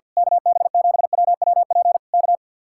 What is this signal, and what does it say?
Morse code, heard as a keyed tone, about 48 wpm, 700 Hz. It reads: KB7WWP K
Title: Speed-Racing Common QSO Phrases - 50wpm